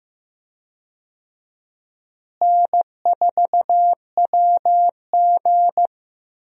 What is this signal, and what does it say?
Morse code, heard as a keyed tone, about 15 wpm, 700 Hz. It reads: N4WG
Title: USA Call Signs - 4 Characters - Repeated 2x - 15wpm